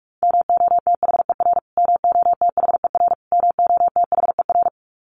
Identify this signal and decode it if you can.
Morse code, heard as a keyed tone, about 44 wpm, 700 Hz. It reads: MOTHER MOTHER MOTHER
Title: Top 300 Words Repeated 45wpm